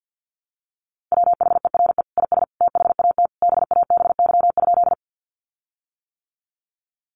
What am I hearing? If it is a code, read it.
Morse code, heard as a keyed tone, about 50 wpm, 700 Hz. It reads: WHERE IS THAT BABY?
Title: Sentences from Top 1300 Words 50wpm